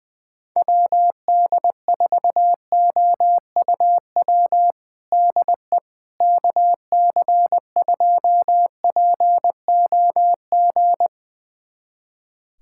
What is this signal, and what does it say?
Morse code, heard as a keyed tone, about 20 wpm, 700 Hz. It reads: WD4OUW DE KC2POG
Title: USA Callsign DE Callsign 20wpm